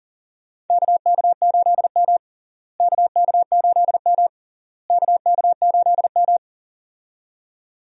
Morse code, heard as a keyed tone, about 40 wpm, 700 Hz. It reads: KK8M KK8M KK8M
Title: USA Call Signs - 4 Characters - Repeated 3x - 40wpm